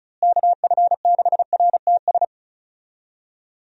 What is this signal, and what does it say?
Morse code, heard as a keyed tone, about 36 wpm, 700 Hz. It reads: KF6RTS
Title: USA Call Signs - 6 Characters - Repeated 2x - 35wpm